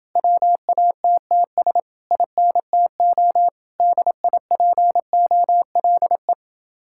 Morse code, heard as a keyed tone, about 28 wpm, 700 Hz. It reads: WATTS INTO DIPOLE